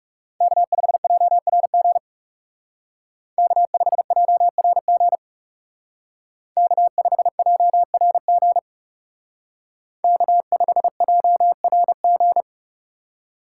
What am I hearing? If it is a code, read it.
Morse code, heard as a keyed tone, about 39 wpm, 700 Hz. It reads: K5JRG K5JRG K5JRG K5JRG